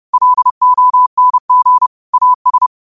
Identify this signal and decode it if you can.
Morse code, heard as a keyed tone, about 30 wpm, 1000 Hz. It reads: LONG AS